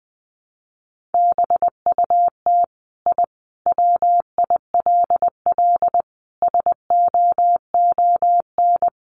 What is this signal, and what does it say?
Morse code, heard as a keyed tone, about 20 wpm, 700 Hz. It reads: BUT I WILL SOON
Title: Sentences from Top 1300 Words 20wpm